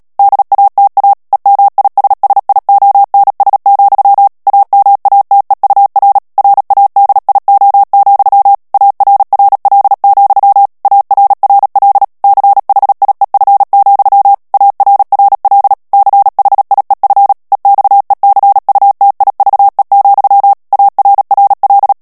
Morse code, heard as a keyed tone, about 37 wpm, 800 Hz. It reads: DATA EMISSIONS, AMATEUR RADIO, ARRL, ARRL CHIEF, ARRL CHIEF EXECUTIVE, ARRL
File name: morse mp3 / ANGLAIS texte clair 3 vitesse 35 mots minute